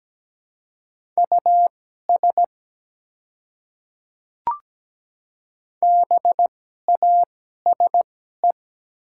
Morse code, heard as a keyed tone, about 17 wpm, spaced as for 12 wpm, 700 Hz. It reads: US BASE
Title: 2x Character Spacing - Top 300 Words - 17wpm